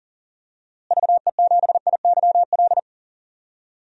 Morse code, heard as a keyed tone, about 40 wpm, 700 Hz. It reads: VE7IYL
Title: DXCC-100 Call Signs - 6 Characters - Repeated 3x - 40wpm